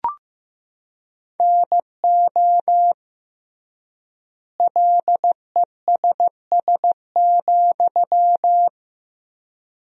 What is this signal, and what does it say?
Morse code, heard as a keyed tone, about 15 wpm, 700 Hz. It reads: NO LESS,
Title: 3x Word-Spacing - Sentences from Top 400 Words - 15wpm